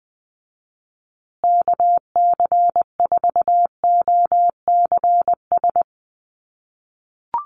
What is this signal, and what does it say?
Morse code, heard as a keyed tone, about 20 wpm, 700 Hz. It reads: KC4OCS
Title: USA Call Signs Repeated 2x 20wpm